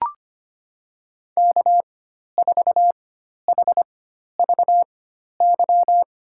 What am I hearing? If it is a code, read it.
Morse code, heard as a keyed tone, about 25 wpm, spaced as for 12 wpm, 700 Hz. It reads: K4HVY
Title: USA Call Signs - 5 Characters - 4x Character Spacing - 25wpm